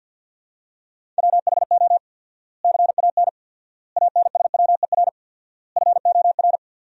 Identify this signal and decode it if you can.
Morse code, heard as a keyed tone, about 50 wpm, 700 Hz. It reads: WHO CAN ANSWER FOR
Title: 4x Word-Spacing - Sentences from Top 300 Words - 50wpm